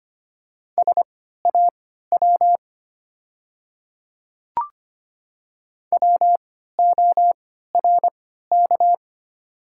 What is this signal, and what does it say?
Morse code, heard as a keyed tone, about 25 wpm, spaced as for 14 wpm, 700 Hz. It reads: SAW WORK
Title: 3x Character Spacing - Top 300 Words - 25wpm